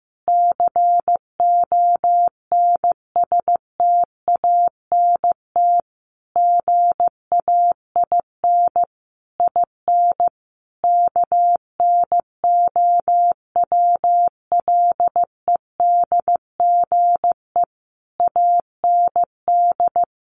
Morse code, heard as a keyed tone, about 15 wpm, 700 Hz. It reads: CONSTANT GAIN IN KNOWLEDGE AND
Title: Sentences from Top 1500 Words 15wpm